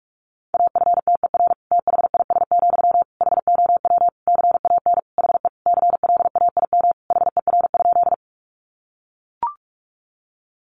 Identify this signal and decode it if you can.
Morse code, heard as a keyed tone, about 44 wpm, 700 Hz. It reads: AFTER THIS, HOW CAN HE CLAIM HER?
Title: Sentences from Top 1500 Words 45wpm